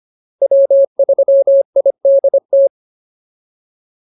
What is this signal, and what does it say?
Morse code, heard as a keyed tone, about 25 wpm, 590 Hz. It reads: W3IDT